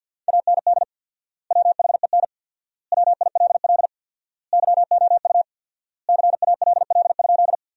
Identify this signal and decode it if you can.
Morse code, heard as a keyed tone, about 50 wpm, 700 Hz. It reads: AND WHEN WILL YOU CALL?
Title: 4x Word-Spacing - Sentences from Top 100 Words - 50wpm